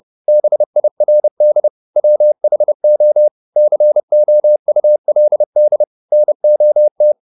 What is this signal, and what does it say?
Morse code, heard as a keyed tone, about 30 wpm, 600 Hz. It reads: BIRD WHO COULD NOT
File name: the old man and the sea / the old man and the sea 1 30wpm